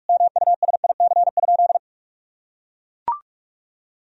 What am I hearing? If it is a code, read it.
Morse code, heard as a keyed tone, about 41 wpm, 700 Hz. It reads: MUSIC?
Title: Sentences from Top 300 Words and Abbreviations 45wpm